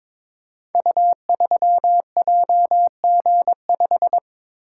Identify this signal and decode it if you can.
Morse code, heard as a keyed tone, about 23 wpm, 700 Hz. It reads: U3JG5
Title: Code Groups - 5 Letter:Numbers - 22wpm